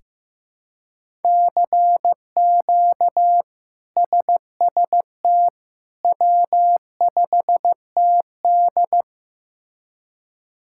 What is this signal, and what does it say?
Morse code, heard as a keyed tone, about 15 wpm, 700 Hz. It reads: CQ SST W5TD
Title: K1USN SST QSO Practice - 15wpm